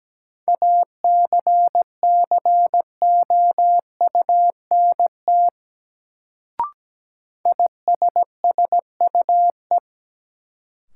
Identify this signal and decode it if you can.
Morse code, heard as a keyed tone, about 17 wpm, 700 Hz. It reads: ACCOUNT ISSUE
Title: Sentences from Top 900 Words - New Word Drill - 17wpm